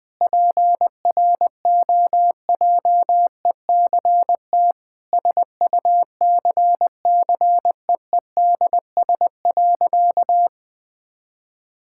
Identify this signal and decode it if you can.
Morse code, heard as a keyed tone, about 20 wpm, 700 Hz. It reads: PROJECT SUCCEEDS.
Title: Sentences with 8-Word Phrases - 20wpm